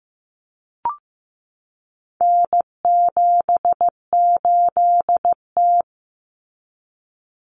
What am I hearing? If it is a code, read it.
Morse code, heard as a keyed tone, about 15 wpm, 700 Hz. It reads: N78T
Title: Common Cut Numbers - 4-Digit 15wpm